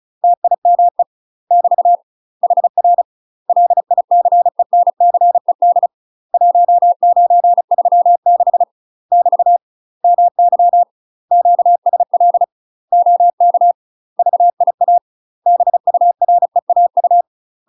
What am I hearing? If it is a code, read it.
Morse code, heard as a keyed tone, about 34 wpm, 700 Hz. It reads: TIME <BT> HR LICENCED 1936 <BT> MY QSL OK VIA BUREAU
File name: UK MASTER QSOs TRAINING / 35WPM / UK MASTER QSOs 0021